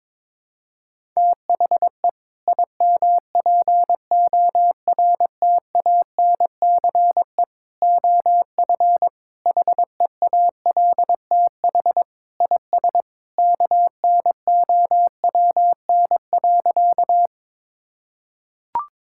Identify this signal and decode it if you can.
Morse code, heard as a keyed tone, about 22 wpm, 700 Hz. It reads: THE IMPORTANCE OF HEALTH IS KNOWN.
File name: practice / 22wpm / Sentences with 3-Word Phrases - 22wpm